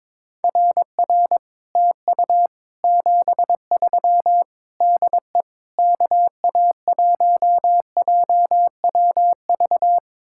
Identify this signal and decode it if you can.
Morse code, heard as a keyed tone, about 22 wpm, 700 Hz. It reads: RR TU 73 DE KA1JWV
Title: Speed-Racing Common QSO Phrases - 22wpm